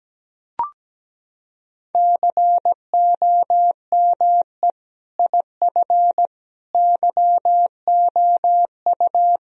Morse code, heard as a keyed tone, about 17 wpm, 700 Hz. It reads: COME IF YOU
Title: Sentences from Top 1000 Words - Review Part 1 - 17wpm